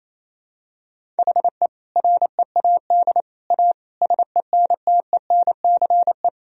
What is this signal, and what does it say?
Morse code, heard as a keyed tone, about 28 wpm, 700 Hz. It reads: HE READ A SENTENCE
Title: Sentences from Top 1000 Words - Review Part 1 - 28wpm